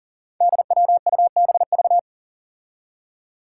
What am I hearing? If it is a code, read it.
Morse code, heard as a keyed tone, about 38 wpm, 700 Hz. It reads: DWUBV
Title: Five Challenging Letter-Number Combinations 40wpm